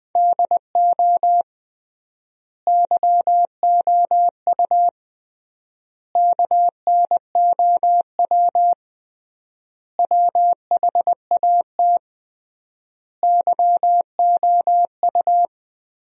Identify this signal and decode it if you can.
Morse code, heard as a keyed tone, about 20 wpm, 700 Hz. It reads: DO YOU KNOW WHAT YOU
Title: 3x Word-Spacing - Sentences from Top 100 Words - 20wpm